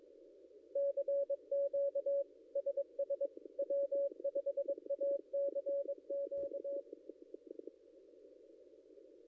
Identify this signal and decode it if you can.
Morse code, heard as a keyed tone, about 22 wpm, 560 Hz. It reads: CQ SS W5ACQ
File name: contests / ss-cw-k5tr-op-n5rz -0800